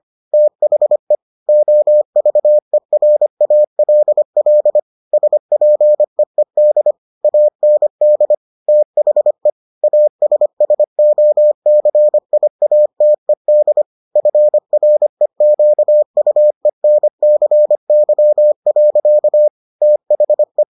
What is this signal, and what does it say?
Morse code, heard as a keyed tone, about 25 wpm, 600 Hz. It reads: THE OVERALL SPEED AND THE ASSOCIATED FREQUENCY. THE